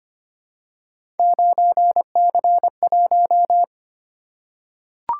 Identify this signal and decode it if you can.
Morse code, heard as a keyed tone, about 25 wpm, 700 Hz. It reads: 9C1
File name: Three Letter-Number Combinations - 25wpm